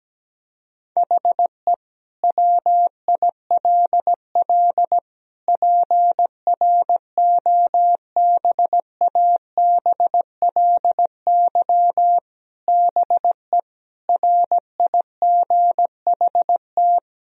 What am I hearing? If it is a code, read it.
Morse code, heard as a keyed tone, about 17 wpm, 700 Hz. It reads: HE WILL PROBABLY BE RIGHT
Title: Sentences from Top 1000 Words - Review Part 4 - 17wpm